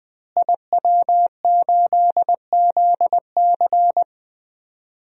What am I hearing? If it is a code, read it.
Morse code, heard as a keyed tone, about 20 wpm, 700 Hz. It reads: IW8ZC
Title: DXCC-100 Call Signs - 5 Characters - Repeated 3x - 20wpm